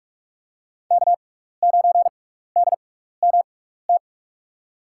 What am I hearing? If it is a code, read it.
Morse code, heard as a keyed tone, about 47 wpm, spaced as for 16 wpm, 700 Hz. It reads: K9DMT
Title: USA Call Signs - 5 Characters - 6x Character Spacing - 45wpm